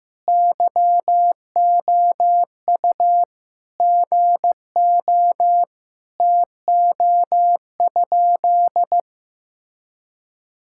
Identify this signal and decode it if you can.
Morse code, heard as a keyed tone, about 16 wpm, 700 Hz. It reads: YOU GO TO?
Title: Sentences from Top 1000 Words - Review Part 5 - 15wpm